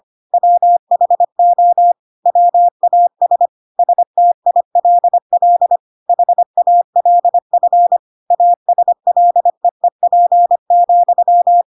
Morse code, heard as a keyed tone, about 25 wpm, 700 Hz. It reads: WHO WAS STILL HALF ASLEEP,
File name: Pinocchio / Pinocchio ch7 25 25 ik1hge